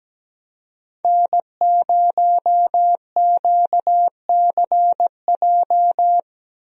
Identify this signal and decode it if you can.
Morse code, heard as a keyed tone, about 17 wpm, 700 Hz. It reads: N0QCJ
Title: USA Call Signs - 5 Characters - 17wpm